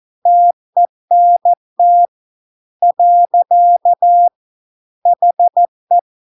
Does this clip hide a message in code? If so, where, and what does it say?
Yes: Morse code from 0.3 to 6.0 s, heard as a keyed tone, about 14 wpm, 700 Hz. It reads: TENT . HE